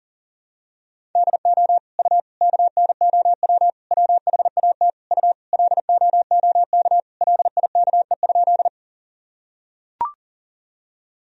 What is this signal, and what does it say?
Morse code, heard as a keyed tone, about 40 wpm, 700 Hz. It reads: DO U KNOW WHAT U LOOK LIKE?
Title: Sentences from Top 100 Words and Abbreviations 40wpm